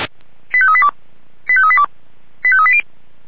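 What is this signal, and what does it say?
Five-tone selective call (ZVEI), heard as quick runs of five tones, tones of about 70 ms. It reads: 85382 85382 85380
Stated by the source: ZVEI five-tone call received on SDR, from feuerwehr-orlamuende.org fire brigade audio server.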